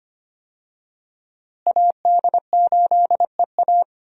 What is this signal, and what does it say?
Morse code, heard as a keyed tone, about 25 wpm, 700 Hz. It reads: AD8EA
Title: USA Call Signs - 5 Characters - Repeated 3x - 25wpm